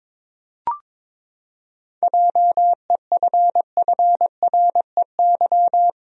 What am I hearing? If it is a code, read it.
Morse code, heard as a keyed tone, about 22 wpm, 700 Hz. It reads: JEFFREY
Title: USA Common Names - 22wpm